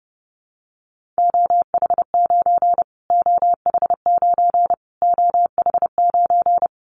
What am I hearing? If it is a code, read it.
Morse code, heard as a keyed tone, about 30 wpm, 700 Hz. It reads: OH9 OH9 OH9
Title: Callsign Prefixes for Top 25 DXCC - Repeated 3x - 30wpm